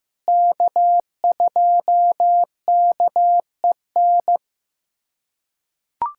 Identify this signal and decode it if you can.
Morse code, heard as a keyed tone, about 15 wpm, 700 Hz. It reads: K2KEN
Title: USA Callsign DE Callsign - 5 Characters - 15wpm